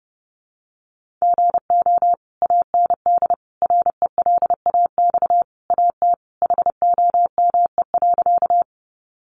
Morse code, heard as a keyed tone, about 30 wpm, 700 Hz. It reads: GO AND RELAX AT HOME.
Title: Sentences from Top 1400 Words 30wpm